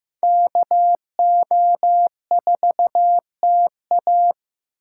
Morse code, heard as a keyed tone, about 18 wpm, 700 Hz. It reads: KO4TA
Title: Speed Racing - USA 2x2 Call Signs 15wpm